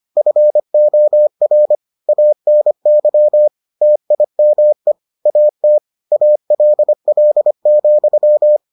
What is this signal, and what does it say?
Morse code, heard as a keyed tone, about 25 wpm, 600 Hz. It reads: FOR ANY TIME AT ALL,